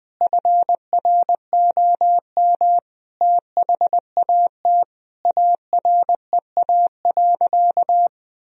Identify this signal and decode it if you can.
Morse code, heard as a keyed tone, about 21 wpm, 700 Hz. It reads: FROM THAT AREA.
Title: Sentences from Top 1000 Words - Review Part 2 - 20wpm